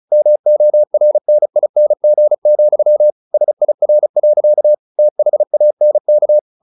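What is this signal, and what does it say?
Morse code, heard as a keyed tone, about 34 wpm, 600 Hz. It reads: MORNING, SIR. THANK
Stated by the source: Morse practice at 35 wpm from CW med Gnister, sending Ken Follett's novel Never in English.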